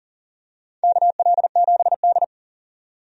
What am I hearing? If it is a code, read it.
Morse code, heard as a keyed tone, about 40 wpm, 700 Hz. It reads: KL7D
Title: DXCC-100 Call Signs - 4 Characters - 40wpm